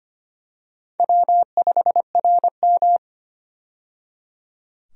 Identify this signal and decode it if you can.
Morse code, heard as a keyed tone, about 25 wpm, 700 Hz. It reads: W5RM
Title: USA Call Signs - 4 Characters - 25wpm